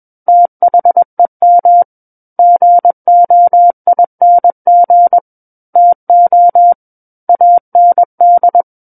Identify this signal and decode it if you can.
Morse code, heard as a keyed tone, about 21 wpm, 700 Hz. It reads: THEM GOING TO AND
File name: Book One / 12-Chapter 12 - 21 WPM